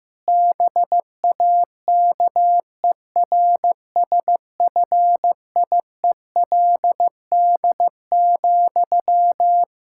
Morse code, heard as a keyed tone, about 15 wpm, 700 Hz. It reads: BAKERSFIELD,
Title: Practice QSO - Exchange 1 - Beginning - 15wpm